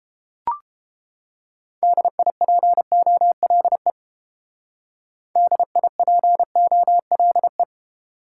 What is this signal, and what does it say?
Morse code, heard as a keyed tone, about 30 wpm, 700 Hz. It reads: DIPOLE DIPOLE